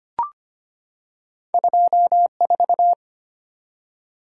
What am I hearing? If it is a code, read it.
Morse code, heard as a keyed tone, about 25 wpm, 700 Hz. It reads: 2 4
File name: Two Letter-Number Combinations - 25wpm